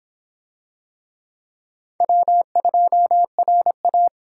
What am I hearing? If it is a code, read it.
Morse code, heard as a keyed tone, about 25 wpm, 700 Hz. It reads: W2RA